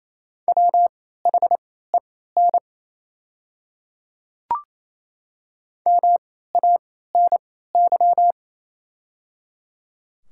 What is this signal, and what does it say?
Morse code, heard as a keyed tone, about 28 wpm, spaced as for 16 wpm, 700 Hz. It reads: WHEN MANY
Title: 3x Character Spacing - Top 300 Words - 28wpm